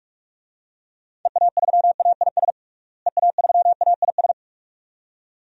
Morse code, heard as a keyed tone, about 46 wpm, 700 Hz. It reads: EA3AIS EA3AIS
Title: DXCC-100 Call Signs - 6 Characters - Repeated 2x - 45wpm